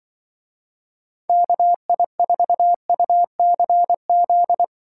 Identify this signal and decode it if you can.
Morse code, heard as a keyed tone, about 25 wpm, 700 Hz. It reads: KI4UCZ
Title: Speed Racing - USA 2x3 Call Signs 20wpm